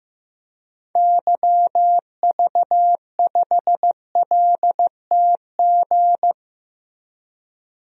Morse code, heard as a keyed tone, about 15 wpm, 700 Hz. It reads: YV5LTG
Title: DXCC-100 Call Signs - 6 Characters - Repeated 3x - 15wpm